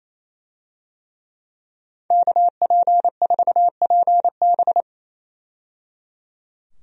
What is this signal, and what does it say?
Morse code, heard as a keyed tone, about 28 wpm, 700 Hz. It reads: KP4PB
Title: DXCC-100 Call Signs - 5 Characters - 28wpm